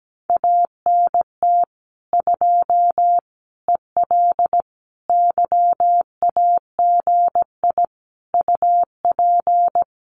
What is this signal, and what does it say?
Morse code, heard as a keyed tone, about 17 wpm, 700 Hz. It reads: ANT 2 EL YAGI UP
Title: Common QSO Phrases Encore 17wpm